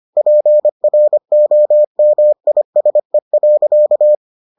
Morse code, heard as a keyed tone, about 25 wpm, 600 Hz. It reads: PROMISE.